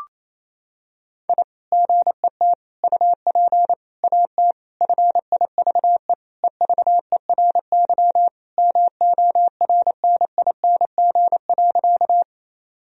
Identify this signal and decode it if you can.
Morse code, heard as a keyed tone, about 28 wpm, 700 Hz. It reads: I GET UP AT FIVE EVERY MORNING.
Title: Sentences from Top 1000 Words - Review Part 2 - 28wpm